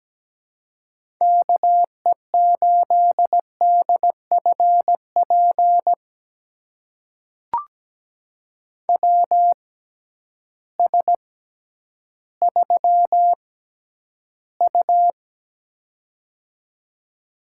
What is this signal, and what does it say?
Morse code, heard as a keyed tone, about 17 wpm, spaced as for 10 wpm, 700 Hz. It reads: KE8DFP WS3U
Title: USA Call Signs - 6x Character Spacing - 17wpm